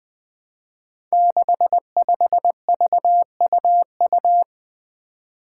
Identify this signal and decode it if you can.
Morse code, heard as a keyed tone, about 20 wpm, 700 Hz. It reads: 65VUU
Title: Five Challenging Letter-Number Combinations 20wpm